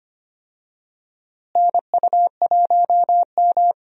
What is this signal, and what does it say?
Morse code, heard as a keyed tone, about 25 wpm, 700 Hz. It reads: NU1M
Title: USA Call Signs - 4 Characters - Repeated 3x - 25wpm